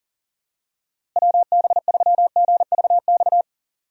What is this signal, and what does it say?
Morse code, heard as a keyed tone, about 40 wpm, 700 Hz. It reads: WB3GVX
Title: USA Call Signs - 6 Characters - Repeated 2x - 40wpm